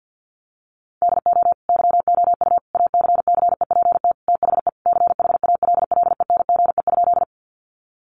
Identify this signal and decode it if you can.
Morse code, heard as a keyed tone, about 50 wpm, 700 Hz. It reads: DO YOU ACCEPT THE CHALLENGE?
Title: Sentences from Top 1300 Words 50wpm